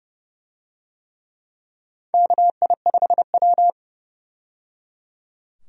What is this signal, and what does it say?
Morse code, heard as a keyed tone, about 30 wpm, 700 Hz. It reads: KI5W
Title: USA Call Signs - 4 Characters - 30wpm